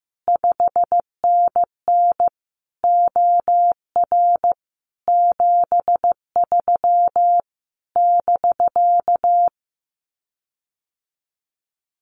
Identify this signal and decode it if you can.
Morse code, heard as a keyed tone, about 15 wpm, 700 Hz. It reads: 5NN OR 73 <BK>
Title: Common QSO Phrases Encore 15wpm